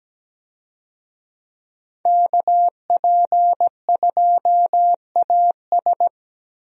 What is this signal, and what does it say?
Morse code, heard as a keyed tone, about 17 wpm, 700 Hz. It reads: KP2AS
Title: USA Call Signs - 5 Characters - Repeated 3x - 17wpm